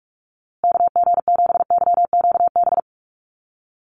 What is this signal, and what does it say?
Morse code, heard as a keyed tone, about 44 wpm, 700 Hz. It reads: KG7YQB
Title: USA Call Signs Repeated 2x 45wpm